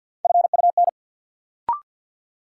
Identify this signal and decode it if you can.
Morse code, heard as a keyed tone, about 53 wpm, 700 Hz. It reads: FUN
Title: Sentences from Top 700 Words - New Word Drill - 50wpm